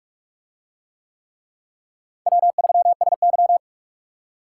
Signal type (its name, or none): Morse code